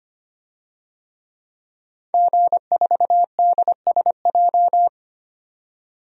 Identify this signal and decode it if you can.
Morse code, heard as a keyed tone, about 25 wpm, 700 Hz. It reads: G4DSJ
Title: DXCC-100 Call Signs - 5 Characters - 25wpm